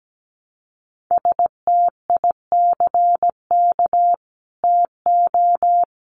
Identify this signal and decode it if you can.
Morse code, heard as a keyed tone, about 17 wpm, 700 Hz. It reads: STICK TO
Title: Sentences from Top 1200 Words 17wpm